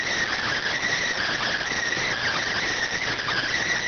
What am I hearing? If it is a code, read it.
Emergency Alert System (SAME) tones, heard as harsh buzzing bursts. It is heard from 0.4 to 3.6 s.